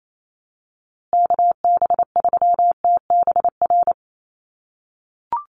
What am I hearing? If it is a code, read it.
Morse code, heard as a keyed tone, about 28 wpm, 700 Hz. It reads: KB3TBR
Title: USA Call Signs Repeated 3x 28wpm